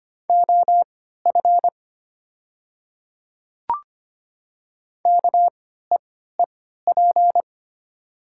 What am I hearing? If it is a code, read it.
Morse code, heard as a keyed tone, about 25 wpm, spaced as for 14 wpm, 700 Hz. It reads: OF KEEP
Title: 3x Character Spacing - Top 300 Words - 25wpm